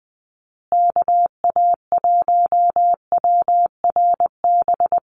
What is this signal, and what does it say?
Morse code, heard as a keyed tone, about 20 wpm, 700 Hz. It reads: KA1WRB
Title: USA Call Signs Repeated 3x 20wpm